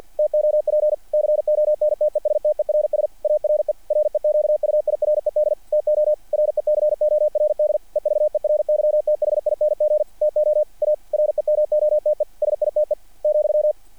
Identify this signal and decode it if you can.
Morse code, heard as a keyed tone, about 48 wpm, 600 Hz. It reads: TOP CONTESTERS ARE REQUIRED TO RECORD EVERYTHING TO A REMOTE SITE ,
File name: archived / CRSnet